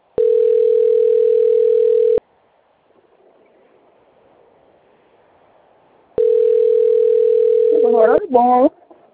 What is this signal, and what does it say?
Ringback tone (North America), 440 + 480 Hz, with speech.
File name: rg-1600-unknown- - - .7330